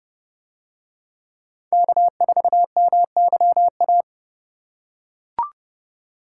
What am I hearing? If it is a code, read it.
Morse code, heard as a keyed tone, about 30 wpm, 700 Hz. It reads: K4MYA